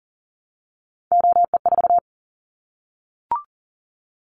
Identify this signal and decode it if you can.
Morse code, heard as a keyed tone, about 38 wpm, 700 Hz. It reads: OE4
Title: Callsign Prefixes for Top 25 DXCC - Repeated 3x - 40wpm